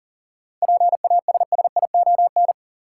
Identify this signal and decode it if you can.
Morse code, heard as a keyed tone, about 37 wpm, 700 Hz. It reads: PASSION
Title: Seven Letter Words - 40wpm